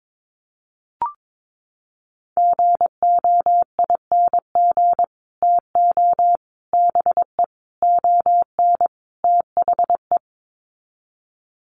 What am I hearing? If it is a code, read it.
Morse code, heard as a keyed tone, about 22 wpm, 700 Hz. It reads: GOING TO BE ON THE
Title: Sets of 5 Words Encore2 22wpm